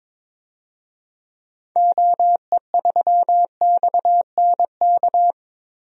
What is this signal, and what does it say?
Morse code, heard as a keyed tone, about 21 wpm, 700 Hz. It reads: OE3XNK
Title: DXCC-100 Call Signs - 6 Characters - Repeated 3x - 22wpm